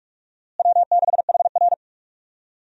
Morse code, heard as a keyed tone, about 43 wpm, 700 Hz. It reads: W6HR
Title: USA Call Signs - 4 Characters - Repeated 3x - 45wpm